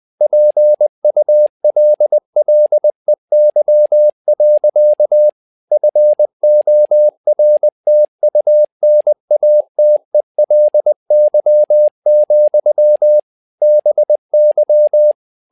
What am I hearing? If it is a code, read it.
Morse code, heard as a keyed tone, about 20 wpm, 600 Hz. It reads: PULLEY. FORTUNATELY, BY
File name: learning center / tower trouble / tower trouble 20wpm